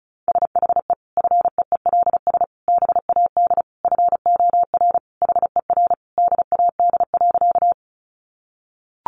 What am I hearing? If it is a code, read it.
Morse code, heard as a keyed tone, about 34 wpm, 700 Hz. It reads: SHE FEELS BAD FOR HER DAD.
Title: Additional Easy Phrases 35wpm